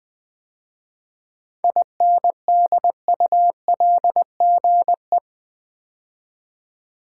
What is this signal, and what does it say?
Morse code, heard as a keyed tone, about 20 wpm, 700 Hz. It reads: INDULGE
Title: 7 Letter Words - First 3 Letters Given - 20wpm